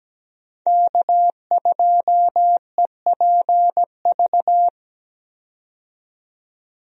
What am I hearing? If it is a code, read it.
Morse code, heard as a keyed tone, about 17 wpm, 700 Hz. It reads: K2EPV
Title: USA Call Signs - 5 Characters - 17wpm